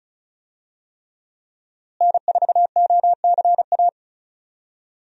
Morse code, heard as a keyed tone, about 36 wpm, 700 Hz. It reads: N4OCA
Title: USA Call Signs - 5 Characters - Repeated 3x - 35wpm